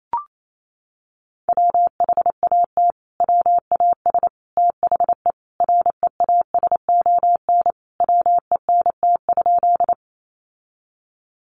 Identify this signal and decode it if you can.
Morse code, heard as a keyed tone, about 28 wpm, 700 Hz. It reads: WHAT WAS THE REASON WENT?
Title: Sentences from Top 700 Words 28wpm